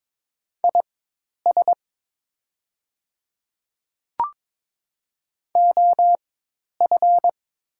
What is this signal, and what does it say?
Morse code, heard as a keyed tone, about 22 wpm, spaced as for 10 wpm, 690 Hz. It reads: IS OF
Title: 4x Character Spacing - Top 100 Words - 22wpm